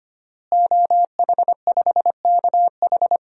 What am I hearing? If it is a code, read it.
Morse code, heard as a keyed tone, about 25 wpm, 700 Hz. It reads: OH5KH
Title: DXCC-100 Call Signs - 5 Characters - Repeated 2x - 25wpm